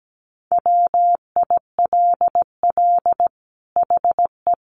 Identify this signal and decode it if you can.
Morse code, heard as a keyed tone, about 17 wpm, 700 Hz. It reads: WILL HE
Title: Sentences from Top 1400 Words 17wpm